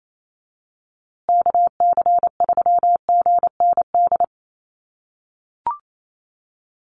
Morse code, heard as a keyed tone, about 28 wpm, 700 Hz. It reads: KC3GND
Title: USA Call Signs Repeated 2x 28wpm